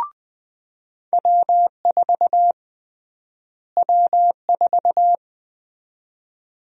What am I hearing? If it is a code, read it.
Morse code, heard as a keyed tone, about 20 wpm, 700 Hz. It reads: W4 W4
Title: USA Call Sign Prefixes - Repeated 2x - 20wpm